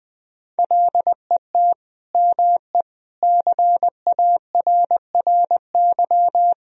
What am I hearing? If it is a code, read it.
Morse code, heard as a keyed tone, about 20 wpm, 700 Hz. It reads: LET ME CARRY
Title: Sentences from Top 1000 Words - Review Part 1 - 20wpm